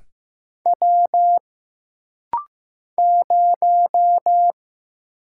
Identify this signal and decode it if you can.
Morse code, heard as a keyed tone, about 15 wpm, 700 Hz. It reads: W 0
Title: Single Letter-Number Alt - Rapid-Fire 15wpm